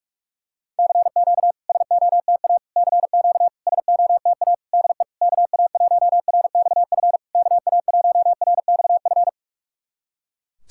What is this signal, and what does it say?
Morse code, heard as a keyed tone, about 46 wpm, 700 Hz. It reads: CQ SOTA CQ SOTA DE KA1RXF KA1RXF
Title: SOTA QSO Practice 45wpm